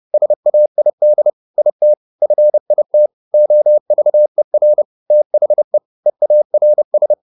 Morse code, heard as a keyed tone, about 30 wpm, 600 Hz. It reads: SAID IT FIT OVER THE EARS